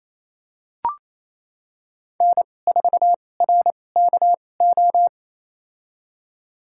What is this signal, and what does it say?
Morse code, heard as a keyed tone, about 28 wpm, spaced as for 20 wpm, 700 Hz. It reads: N4RKO
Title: USA Call Signs - 5 Characters - 2x Character Spacing - 28wpm